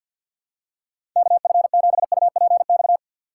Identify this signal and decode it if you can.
Morse code, heard as a keyed tone, about 52 wpm, 700 Hz. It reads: KF7UPX
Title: USA Call Signs - 6 Characters - 6x Character Spacing - 50wpm